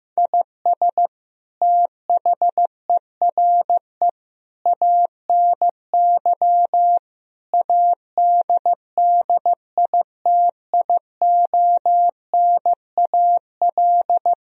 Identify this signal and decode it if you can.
Morse code, heard as a keyed tone, about 16 wpm, 700 Hz. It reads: IS THERE ANY ADDITIONAL
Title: Sentences from Top 1000 Words - Review Part 4 - 15wpm